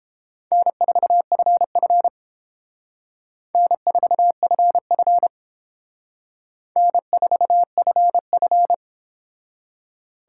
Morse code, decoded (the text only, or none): N4FF N4FF N4FF